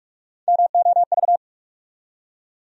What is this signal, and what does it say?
Morse code, heard as a keyed tone, about 46 wpm, 700 Hz. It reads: MOV
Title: USA Call Sign Postfixes - 3 Letters 45wpm